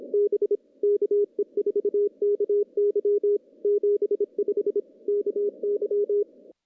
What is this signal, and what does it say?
Morse code, heard as a keyed tone, about 26 wpm, 410 Hz. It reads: B KE4KY 75 KY
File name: contests / CW-SS-k5tr- -1500